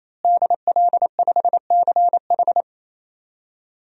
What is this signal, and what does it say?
Morse code, heard as a keyed tone, about 28 wpm, 700 Hz. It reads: DL5CH
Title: DXCC-100 Call Signs - 5 Characters - Repeated 3x - 28wpm